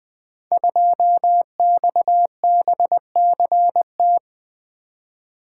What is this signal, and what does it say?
Morse code, heard as a keyed tone, about 20 wpm, 700 Hz. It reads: 2XBCT
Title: Code Groups - 5 Letter:Numbers - 20wpm